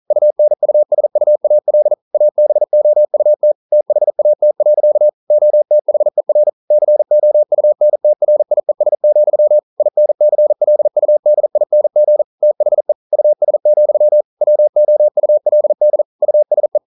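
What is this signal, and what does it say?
Morse code, heard as a keyed tone, about 42 wpm, 600 Hz. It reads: UNUSUAL ABOUT THAT. OTHER COUNTRIES, INCLUDING THE US, WOULD USE
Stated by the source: Morse code practice at 41 wpm, from CW med Gnister.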